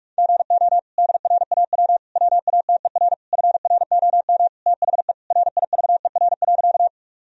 Morse code, heard as a keyed tone, about 46 wpm, 700 Hz. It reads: GO DRAW WATER FROM THE RIVER.
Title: Sentences from Top 1000 Words - Review Part 1 - 45wpm